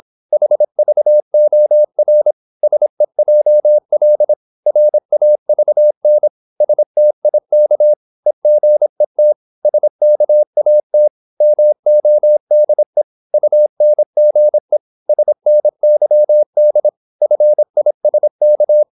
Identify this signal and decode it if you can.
Morse code, heard as a keyed tone, about 26 wpm, 600 Hz. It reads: HVOR SEJL RAVN STIK EGET SKAT MODE UNGE SNYD FISK
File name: Korte ord DK 26wpm